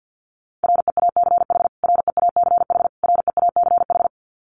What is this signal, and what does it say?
Morse code, heard as a keyed tone, about 50 wpm, 700 Hz. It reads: REACH REACH REACH
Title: Top 500 Words Repeated 50wpm